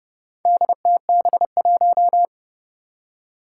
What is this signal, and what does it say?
Morse code, heard as a keyed tone, about 30 wpm, 700 Hz. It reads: DTB1
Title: Four Challenging Letter-Number Combinations 30wpm